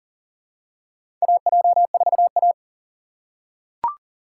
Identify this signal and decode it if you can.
Morse code, heard as a keyed tone, about 39 wpm, 700 Hz. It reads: AJ4A
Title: USA Call Signs - 4 Characters - 6x Character Spacing - 40wpm